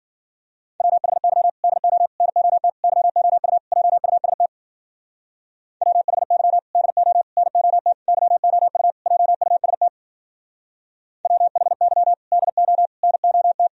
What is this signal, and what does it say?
Morse code, heard as a keyed tone, about 51 wpm, 700 Hz. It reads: WHY DO NOT YOU JUST WHY DO NOT YOU JUST WHY DO NOT